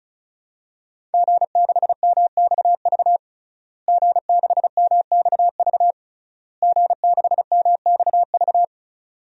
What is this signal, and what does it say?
Morse code, heard as a keyed tone, about 36 wpm, 700 Hz. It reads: G6MXV G6MXV G6MXV
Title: DXCC-100 Call Signs - 5 Characters - Repeated 3x - 35wpm